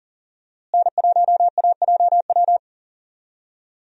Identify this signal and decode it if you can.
Morse code, heard as a keyed tone, about 40 wpm, 700 Hz. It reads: N1AJW
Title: USA Call Signs - 5 Characters - Repeated 2x - 40wpm